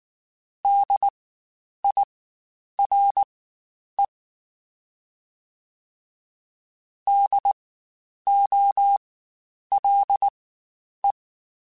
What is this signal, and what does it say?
Morse code, heard as a keyed tone, about 19 wpm, spaced as for 9 wpm, 800 Hz. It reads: DIRE DOLE
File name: Spaced Alphas / Spaced Alphas mp3s / Ds sp @18